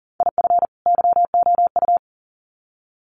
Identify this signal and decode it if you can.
Morse code, heard as a keyed tone, about 40 wpm, 700 Hz. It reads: IF YOU
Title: Sets of 2 Words 40wpm